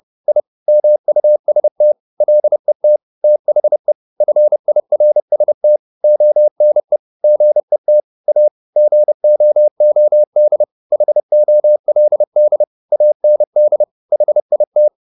Morse code, heard as a keyed tone, about 30 wpm, 600 Hz. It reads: I MUST LET THE FIRST ONE GET A GOOD HOLD AND HIT